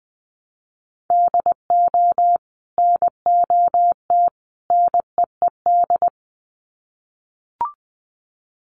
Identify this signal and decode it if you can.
Morse code, heard as a keyed tone, about 20 wpm, 700 Hz. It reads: DO NOT NEED
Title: Sets of 3 Words 20wpm